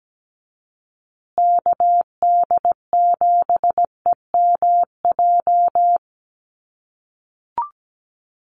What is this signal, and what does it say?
Morse code, heard as a keyed tone, about 17 wpm, 700 Hz. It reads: KD7EMJ
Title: USA Call Signs Repeated 3x 17wpm